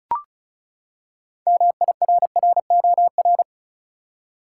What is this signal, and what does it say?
Morse code, heard as a keyed tone, about 36 wpm, 700 Hz. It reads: MIRROR
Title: Six Letter Words - 35wpm